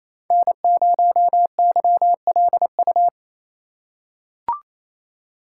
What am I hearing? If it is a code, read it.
Morse code, heard as a keyed tone, about 28 wpm, 700 Hz. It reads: N0YLU
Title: USA Call Signs - 5 Characters - 2x Character Spacing - 28wpm